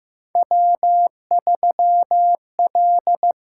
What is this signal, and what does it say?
Morse code, heard as a keyed tone, about 15 wpm, 700 Hz. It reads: W3L
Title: USA Call Signs - 8x Character Spacing - 15wpm